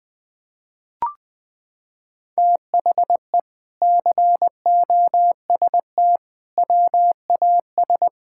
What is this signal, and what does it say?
Morse code, heard as a keyed tone, about 20 wpm, 700 Hz. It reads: THE COST WAS
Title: Sentences from Top 1000 Words - Review Part 6 - 20wpm